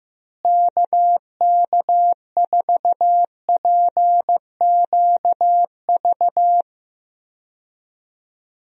Morse code, heard as a keyed tone, about 16 wpm, 700 Hz. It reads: KK4PQV
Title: USA Call Signs - 6 Characters - Repeated 2x - 15wpm